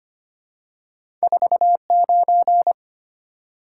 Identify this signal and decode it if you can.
Morse code, heard as a keyed tone, about 25 wpm, 700 Hz. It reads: 49
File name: Two Letter-Number Combinations - 25wpm